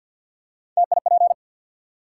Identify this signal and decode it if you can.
Morse code, heard as a keyed tone, about 53 wpm, 700 Hz. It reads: TIP
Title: Three Letter Words - 50wpm